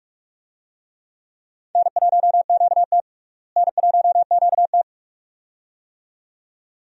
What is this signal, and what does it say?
Morse code, heard as a keyed tone, about 41 wpm, 700 Hz. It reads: N1QT N1QT
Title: USA Call Signs - 4 Characters - Repeated 2x - 45wpm